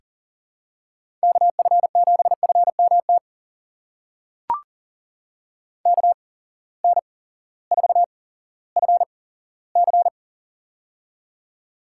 Morse code, heard as a keyed tone, about 40 wpm, spaced as for 18 wpm, 700 Hz. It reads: KF7FMT KN4FC
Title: USA Call Signs - 8x Character Spacing - 40wpm